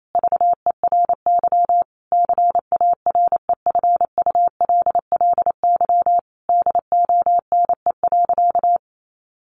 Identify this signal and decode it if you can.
Morse code, heard as a keyed tone, about 28 wpm, 700 Hz. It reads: VERY CAREFULLY DONE.
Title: Sentences from Top 1300 Words 28wpm